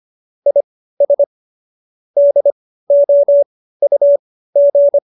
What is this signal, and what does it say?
Morse code, heard as a keyed tone, about 25 wpm, spaced as for 15 wpm, 570 Hz. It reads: IS DOUG